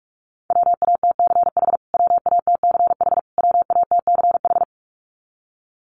Morse code, heard as a keyed tone, about 44 wpm, 700 Hz. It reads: WATCH WATCH WATCH
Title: Top 500 Words Repeated 45wpm